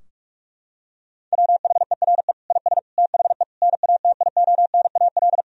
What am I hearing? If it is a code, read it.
Morse code, heard as a keyed tone, about 46 wpm, 700 Hz. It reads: WHERE IS THE NATIONAL
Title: Sentences from Top 1000 Words - Review Part 4 - 45wpm